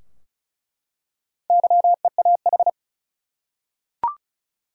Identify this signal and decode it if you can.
Morse code, heard as a keyed tone, about 35 wpm, 700 Hz. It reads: YEAH
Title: Four Letter Words - 35wpm